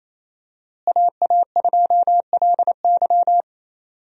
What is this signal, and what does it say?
Morse code, heard as a keyed tone, about 28 wpm, 700 Hz. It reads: AA2LY
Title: USA Call Signs - 5 Characters - Repeated 3x - 28wpm